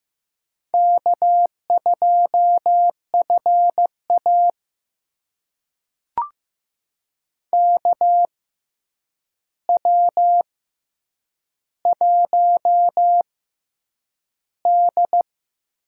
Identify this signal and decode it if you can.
Morse code, heard as a keyed tone, about 15 wpm, spaced as for 8 wpm, 700 Hz. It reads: K2FA KW1D
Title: USA Call Signs - 6x Character Spacing - 15wpm